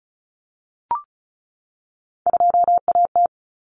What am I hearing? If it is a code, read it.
Morse code, heard as a keyed tone, about 34 wpm, 700 Hz. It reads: 2AT
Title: Common Cut Numbers - 3-Digit 35wpm